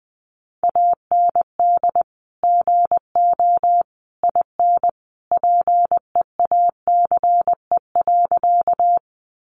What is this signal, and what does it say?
Morse code, heard as a keyed tone, about 19 wpm, 700 Hz. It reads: AND GO IN PEACE.
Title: Sentences from Top 1500 Words 20wpm